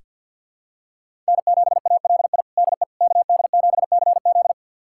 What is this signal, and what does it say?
Morse code, heard as a keyed tone, about 47 wpm, 700 Hz. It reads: N7ALI DE KD7CZ
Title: USA Callsign DE Callsign - 5 Characters - 50wpm